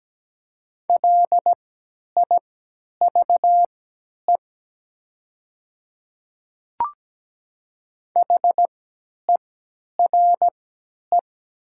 Morse code, heard as a keyed tone, about 17 wpm, spaced as for 10 wpm, 700 Hz. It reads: LIVE HERE
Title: 3x Character Spacing - Top 200 Words - 17wpm